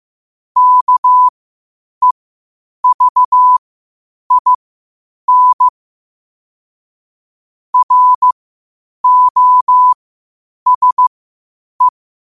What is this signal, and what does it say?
Morse code, heard as a keyed tone, about 15 wpm, spaced as for 8 wpm, 1000 Hz. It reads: KEVIN ROSE
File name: qotdfiles / 8wpm